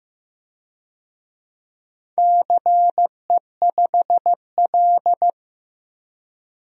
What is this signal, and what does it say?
Morse code, heard as a keyed tone, about 15 wpm, 700 Hz. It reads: CE5L
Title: DXCC-100 Call Signs - 4 Characters - 15wpm